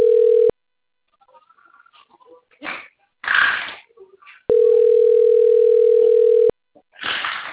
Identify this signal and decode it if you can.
Ringback tone (North America), 440 + 480 Hz.